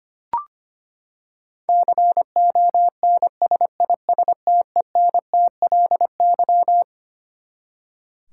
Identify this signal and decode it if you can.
Morse code, heard as a keyed tone, about 25 wpm, 700 Hz. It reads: CONSISTENTLY
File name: Twelve Letter Words - 25wpm